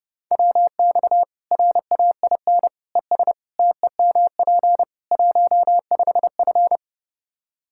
Morse code, heard as a keyed tone, about 30 wpm, 700 Hz. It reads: WX RAIN ES TEMP 15F
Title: Practice QSO - Exchange 2 - Info - 30wpm